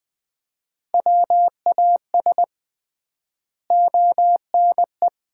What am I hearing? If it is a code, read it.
Morse code, heard as a keyed tone, about 20 wpm, 700 Hz. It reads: WAS ONE
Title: 3x Word-Spacing - Sets of 4 Words - 20wpm